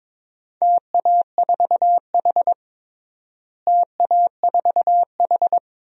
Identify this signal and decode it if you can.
Morse code, heard as a keyed tone, about 22 wpm, 700 Hz. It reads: TA4H TA4H
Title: DXCC-100 Call Signs - 4 Characters - Repeated 3x - 22wpm